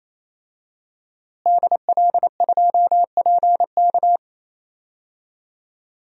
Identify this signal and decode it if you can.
Morse code, heard as a keyed tone, about 27 wpm, 700 Hz. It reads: DL2PK
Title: DXCC-100 Call Signs - 5 Characters - Repeated 3x - 28wpm